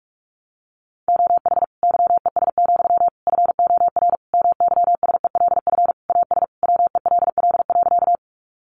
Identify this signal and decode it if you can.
Morse code, heard as a keyed tone, about 44 wpm, 700 Hz. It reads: OH YES, FOR MYSELF AS WELL.
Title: Sentences from Top 1300 Words 45wpm